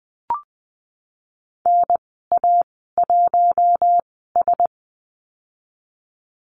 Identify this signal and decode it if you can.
Morse code, heard as a keyed tone, about 19 wpm, spaced as for 14 wpm, 700 Hz. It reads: NA1S
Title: 076 Lesson 13g - Letter L - Call Signs 20wpm